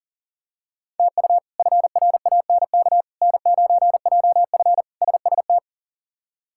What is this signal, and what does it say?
Morse code, heard as a keyed tone, about 37 wpm, 700 Hz. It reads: TU FRANK N9JF SST
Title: K1USN SST QSO Practice - 40wpm